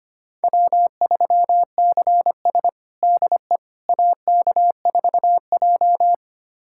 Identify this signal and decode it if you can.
Morse code, heard as a keyed tone, about 25 wpm, 700 Hz. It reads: W3CS DE AK4J
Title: USA Callsign DE Callsign - 4 Characters - 25wpm